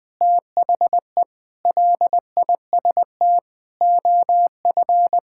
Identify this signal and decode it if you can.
Morse code, heard as a keyed tone, about 20 wpm, 700 Hz. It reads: THE LIST OF
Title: Sentences with 7-Word Phrases - 20wpm